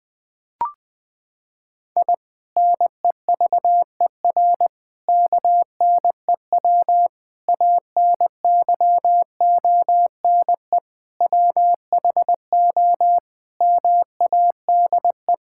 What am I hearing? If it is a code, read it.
Morse code, heard as a keyed tone, about 20 wpm, 700 Hz. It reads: I NEVER KNEW ANYONE WHO MADE
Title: Sentences from Top 1000 Words - Review Part 6 - 20wpm